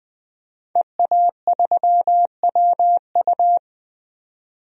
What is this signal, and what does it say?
Morse code, heard as a keyed tone, about 20 wpm, 700 Hz. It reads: EA3WU
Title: DXCC-100 Call Signs - 5 Characters - 20wpm